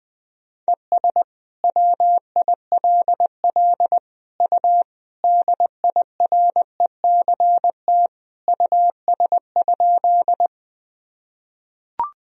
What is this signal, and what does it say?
Morse code, heard as a keyed tone, about 20 wpm, 700 Hz. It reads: ES WILL U DIRECT US?
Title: Sentences from Top 400 Words and Abbreviations 20wpm